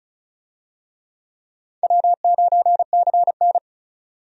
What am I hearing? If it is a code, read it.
Morse code, heard as a keyed tone, about 36 wpm, 700 Hz. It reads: W9CN
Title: USA Call Signs - 4 Characters - 35wpm